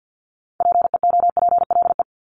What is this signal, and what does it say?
Morse code, heard as a keyed tone, about 50 wpm, 700 Hz. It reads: PEOPLE
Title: Multi-letter Phonemes Repeated 3x 50wpm